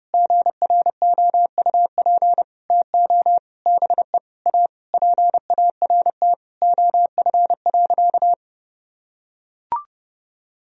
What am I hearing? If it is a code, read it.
Morse code, heard as a keyed tone, about 30 wpm, 700 Hz. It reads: GROUP TO BE A PART OF.
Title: Sentences from Top 1000 Words - Review Part 1 - 30wpm